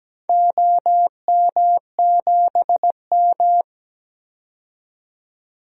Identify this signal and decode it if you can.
Morse code, heard as a keyed tone, about 17 wpm, 700 Hz. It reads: OM7M
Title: DXCC-100 Call Signs - 4 Characters - 17wpm